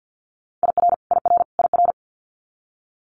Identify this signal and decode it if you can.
Morse code, heard as a keyed tone, about 50 wpm, 700 Hz. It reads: IR IR IR
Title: Multi-letter Phonemes Repeated 3x 50wpm